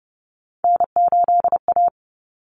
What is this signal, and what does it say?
Morse code, heard as a keyed tone, about 30 wpm, 700 Hz. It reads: N8A
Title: Common Cut Numbers - 3-Digit 30wpm